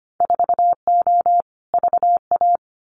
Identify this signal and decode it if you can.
Morse code, heard as a keyed tone, about 25 wpm, 700 Hz. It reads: 4O VA
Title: Winter Field Day Exchanges Encore 25wpm